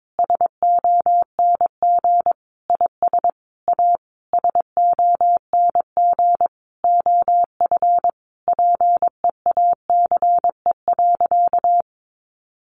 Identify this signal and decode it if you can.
Morse code, heard as a keyed tone, about 22 wpm, 700 Hz. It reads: SONG IS A SONG OF PEACE.
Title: Sentences from Top 1500 Words 22wpm